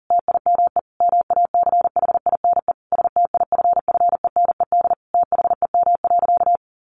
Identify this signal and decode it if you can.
Morse code, heard as a keyed tone, about 37 wpm, 700 Hz. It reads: TIME MACHINE STIFFENED THEM.
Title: A Sound Of Thunder 40wpm